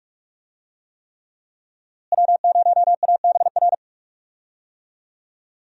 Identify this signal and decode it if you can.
Morse code, heard as a keyed tone, about 44 wpm, 700 Hz. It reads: W0ABR
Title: USA Call Signs - 5 Characters - Repeated 3x - 45wpm